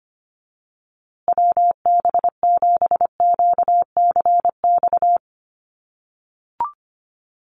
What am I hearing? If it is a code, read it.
Morse code, heard as a keyed tone, about 25 wpm, 700 Hz. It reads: WB7QCX
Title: USA Call Signs Repeated 2x 25wpm